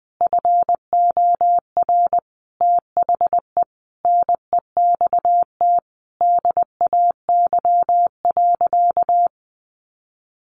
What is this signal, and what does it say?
Morse code, heard as a keyed tone, about 20 wpm, 700 Hz. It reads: FOR THE NEXT DAY.
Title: Sentences from Top 1200 Words 20wpm